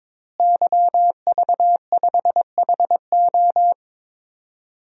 Morse code, heard as a keyed tone, about 23 wpm, 700 Hz. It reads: YV5HO
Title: DXCC-100 Call Signs - 5 Characters - Repeated 2x - 22wpm